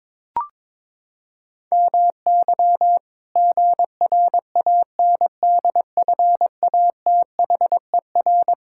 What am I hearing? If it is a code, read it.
Morse code, heard as a keyed tone, about 22 wpm, 700 Hz. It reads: MY GRANDFATHER
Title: Pangram Challenge v2 - 22wpm